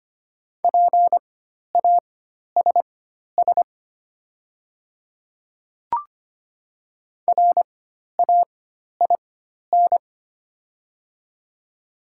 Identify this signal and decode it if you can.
Morse code, heard as a keyed tone, about 25 wpm, spaced as for 12 wpm, 700 Hz. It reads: PASS RAIN
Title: 4x Character Spacing - Top 500 Words - 25wpm